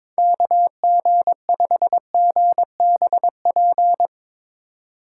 Morse code, heard as a keyed tone, about 24 wpm, 700 Hz. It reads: KG5GBP
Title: Speed Racing - USA 2x3 Call Signs 22wpm